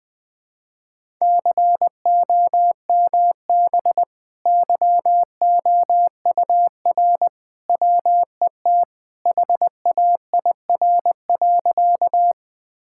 Morse code, heard as a keyed tone, about 21 wpm, 700 Hz. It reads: COMB YOUR WET HAIR.
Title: Easy Sentences with Hints - 20wpm